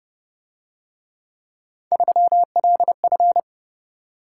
Morse code, heard as a keyed tone, about 28 wpm, 700 Hz. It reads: 3LF